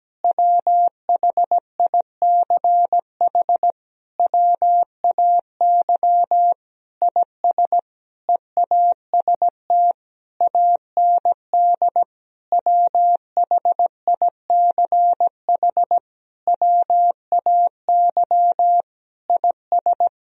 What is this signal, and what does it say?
Morse code, heard as a keyed tone, about 17 wpm, 700 Hz. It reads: WHICH WAY IS EAST AND WHICH WAY IS
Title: Sentences from Top 1000 Words - Review Part 3 - 17wpm